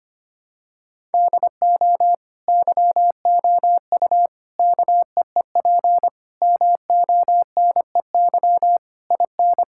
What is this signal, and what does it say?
Morse code, heard as a keyed tone, about 25 wpm, 700 Hz. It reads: DO YOU KEEP MONEY IN
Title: Sentences from Top 1000 Words - Review Part 6 - 25wpm